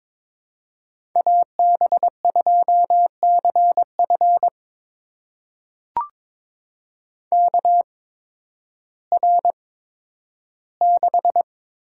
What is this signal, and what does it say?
Morse code, heard as a keyed tone, about 22 wpm, spaced as for 12 wpm, 700 Hz. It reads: AB2CF KR6
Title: USA Call Signs - 5 Characters - 8x Character Spacing - 22wpm